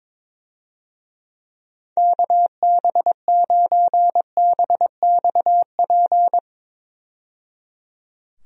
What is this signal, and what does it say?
Morse code, heard as a keyed tone, about 22 wpm, 700 Hz. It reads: KB9BXP
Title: USA Call Signs - 6 Characters - 22wpm